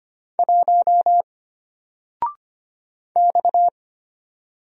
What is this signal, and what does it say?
Morse code, heard as a keyed tone, about 25 wpm, 700 Hz. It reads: 1 X
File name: Single-Letter-Number-25wpm